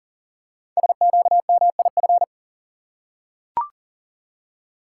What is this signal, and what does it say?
Morse code, heard as a keyed tone, about 40 wpm, 700 Hz. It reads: SQMIF
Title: Code Groups - 5 Letters - 40wpm